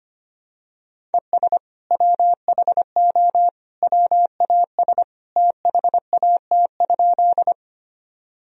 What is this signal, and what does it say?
Morse code, heard as a keyed tone, about 25 wpm, 700 Hz. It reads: ES WHO WAS THAT?
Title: Sentences from Top 100 Words and Abbreviations 25wpm